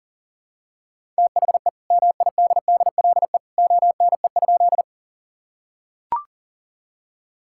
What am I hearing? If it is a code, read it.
Morse code, heard as a keyed tone, about 40 wpm, 700 Hz. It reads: THE MIDDLE ONE?
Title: Sentences from Top 1000 Words - Review Part 6 - 40wpm